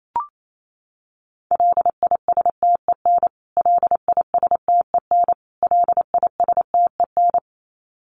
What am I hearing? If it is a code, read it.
Morse code, heard as a keyed tone, about 28 wpm, 700 Hz. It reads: LISTEN LISTEN LISTEN
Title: Top 500 Words Repeated 28wpm